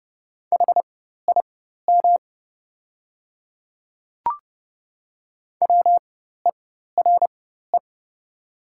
Morse code, heard as a keyed tone, about 30 wpm, spaced as for 14 wpm, 690 Hz. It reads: HIM WERE
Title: 4x Character Spacing - Top 200 Words - 30wpm